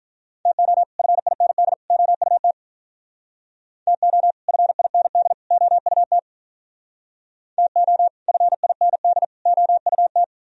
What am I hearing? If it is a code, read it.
Morse code, heard as a keyed tone, about 42 wpm, 700 Hz. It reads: TO FIND OUT TO FIND OUT TO FIND OUT
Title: Speed-Racing - Sets of 3 Words 35wpm